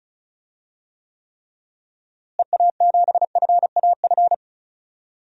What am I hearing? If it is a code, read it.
Morse code, heard as a keyed tone, about 34 wpm, 700 Hz. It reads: EA7FAF
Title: DXCC-100 Call Signs - 6 Characters - Repeated 3x - 35wpm